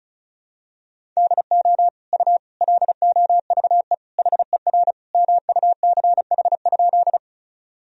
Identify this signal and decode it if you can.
Morse code, heard as a keyed tone, about 34 wpm, 700 Hz. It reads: DO U LOVE HER MUCH?
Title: Sentences from Top 400 Words and Abbreviations 35wpm